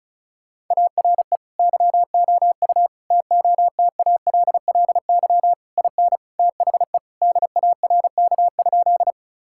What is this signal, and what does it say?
Morse code, heard as a keyed tone, about 34 wpm, 700 Hz. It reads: ARE YOU TOTALLY IN THE DARK?
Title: Sentences from Top 1600 Words 35wpm